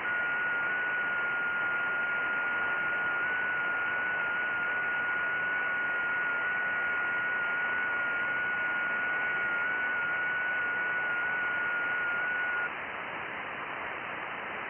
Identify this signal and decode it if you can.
FT8.
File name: IC-7100